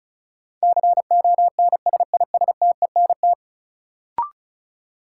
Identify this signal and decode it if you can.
Morse code, heard as a keyed tone, about 36 wpm, 700 Hz. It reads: CONSISTENT
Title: Sentences from Top 1500 Words - New Word Drill Alt - 35wpm